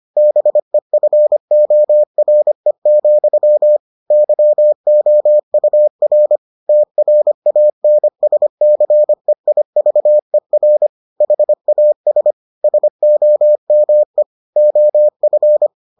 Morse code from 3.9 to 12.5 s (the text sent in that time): YOUR TRANSCEIVER HAS